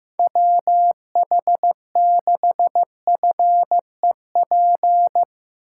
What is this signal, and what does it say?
Morse code, heard as a keyed tone, about 15 wpm, 700 Hz. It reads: WH6FEP
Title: USA Call Signs - 6 Characters - 4x Character Spacing - 15wpm